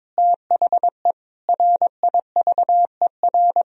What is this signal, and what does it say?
Morse code, heard as a keyed tone, about 22 wpm, 700 Hz. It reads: THE RIVER
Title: Sentences from Top 1000 Words - Review Part 3 - 22wpm